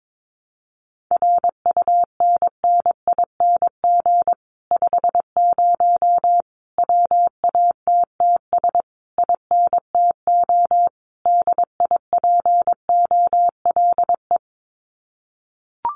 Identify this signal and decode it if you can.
Morse code, heard as a keyed tone, about 22 wpm, 700 Hz. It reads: RUNNING 50 WATTS INTO DIPOLE
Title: Common QSO Phrases Encore 22wpm